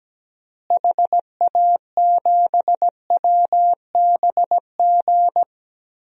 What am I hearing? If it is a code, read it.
Morse code, heard as a keyed tone, about 17 wpm, 700 Hz. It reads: HA7WBG
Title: DXCC-100 Call Signs - 6 Characters - Repeated 2x - 17wpm